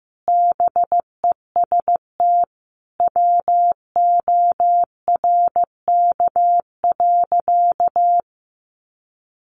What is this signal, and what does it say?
Morse code, heard as a keyed tone, about 15 wpm, 700 Hz. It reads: BEST WORK.
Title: Sentences from Top 1200 Words 15wpm